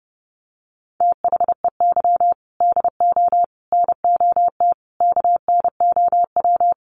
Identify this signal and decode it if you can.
Morse code, heard as a keyed tone, about 30 wpm, 700 Hz. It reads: THEY DO NOT KNOW
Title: Sets of 5 Words Encore 30wpm